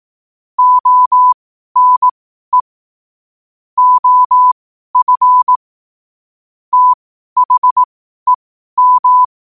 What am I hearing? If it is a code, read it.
Morse code, heard as a keyed tone, about 18 wpm, spaced as for 12 wpm, 1000 Hz. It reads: ONE OF THEM
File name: qotdfiles / 12wpm